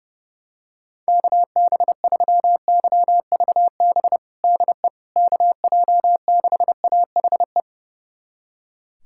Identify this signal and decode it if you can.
Morse code, heard as a keyed tone, about 30 wpm, 700 Hz. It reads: KB3YVB DE KJ6AHE
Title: USA Callsign DE Callsign - 6 Characters - 30wpm